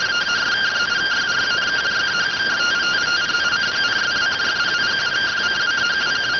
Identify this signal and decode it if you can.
Two-tone teleprinter signal, about 170 Hz shift.